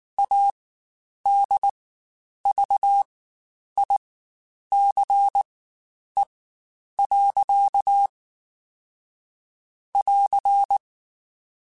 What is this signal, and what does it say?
Morse code, heard as a keyed tone, about 19 wpm, spaced as for 9 wpm, 800 Hz. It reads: ADVICE. <AR>
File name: Quotes Spaced / Quotes Spaced-mp3s / Advice sp @18